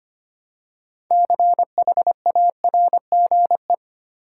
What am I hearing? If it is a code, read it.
Morse code, heard as a keyed tone, about 25 wpm, 700 Hz. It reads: CHARGE
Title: Six Letter Words - 25wpm